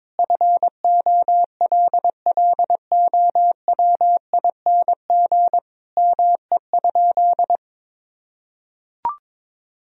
Morse code, heard as a keyed tone, about 22 wpm, 700 Hz. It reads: FOLLOWING ME?
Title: Sentences from Top 1000 Words - Review Part 4 - 22wpm